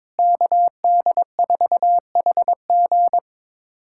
Morse code, heard as a keyed tone, about 23 wpm, 700 Hz. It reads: KD4HG
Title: USA Callsign DE Callsign 22wpm